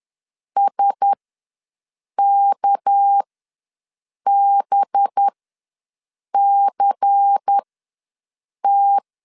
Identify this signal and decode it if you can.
Morse code, heard as a keyed tone, about 10 wpm, spaced as for 6 wpm, 800 Hz. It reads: SKBCT
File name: LEZIONI / 6 PKX0 / 6 PKX ws 10wpm 3706